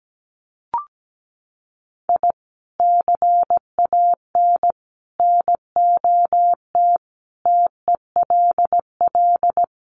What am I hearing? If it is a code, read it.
Morse code, heard as a keyed tone, about 17 wpm, 700 Hz. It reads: I CAN NOT TELL
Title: Sets of 5 Words 17wpm